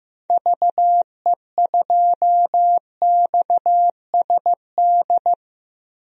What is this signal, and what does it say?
Morse code, heard as a keyed tone, about 15 wpm, 700 Hz. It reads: VE2XSD
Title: DXCC-100 Call Signs - 6 Characters - 15wpm